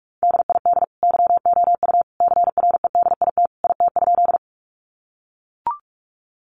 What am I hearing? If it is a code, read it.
Morse code, heard as a keyed tone, about 44 wpm, 700 Hz. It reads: DID YOU CREDIT IT?
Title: Sentences from Top 1000 Words 45wpm